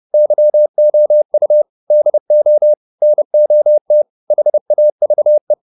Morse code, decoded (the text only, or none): YOU DO NOT HAVE